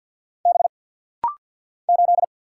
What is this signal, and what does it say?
Morse code, heard as a keyed tone, about 49 wpm, 700 Hz. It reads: B 8
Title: Single Letter-Number - Rapid-Fire 50wpm